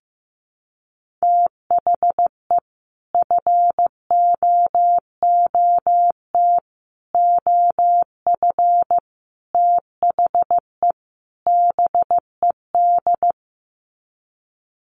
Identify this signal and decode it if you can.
Morse code, heard as a keyed tone, about 15 wpm, 700 Hz. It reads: THE FOOT OF THE BED
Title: Sets of 5 Words Encore2 15wpm